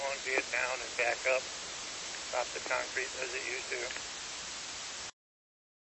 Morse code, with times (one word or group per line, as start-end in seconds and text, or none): none